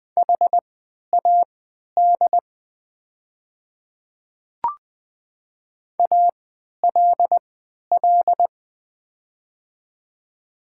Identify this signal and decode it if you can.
Morse code, heard as a keyed tone, about 21 wpm, spaced as for 12 wpm, 700 Hz. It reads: HAD ALL
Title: 3x Character Spacing - Top 100 Words - 20wpm